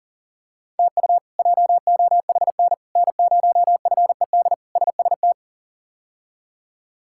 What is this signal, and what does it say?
Morse code, heard as a keyed tone, about 40 wpm, 700 Hz. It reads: TU JOHN N0FED SST
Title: K1USN SST QSO Practice - 40wpm